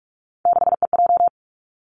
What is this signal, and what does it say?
Morse code, heard as a keyed tone, about 46 wpm, 700 Hz. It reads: 6EJ
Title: Three Challenging Letter-Number Combinations 45wpm